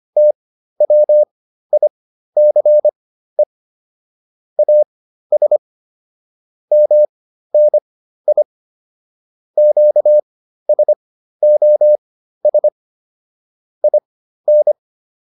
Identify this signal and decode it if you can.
Morse code, heard as a keyed tone, about 25 wpm, spaced as for 13 wpm, 600 Hz. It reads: TWICE AS MNI QSOS IN